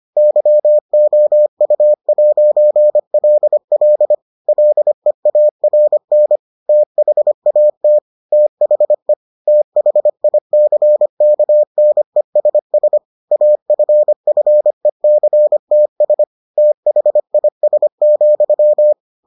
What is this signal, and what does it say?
Morse code, heard as a keyed tone, about 25 wpm, 600 Hz. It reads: YOU'LL LEARN THAT THE THICKNESS AFFECTS THIS,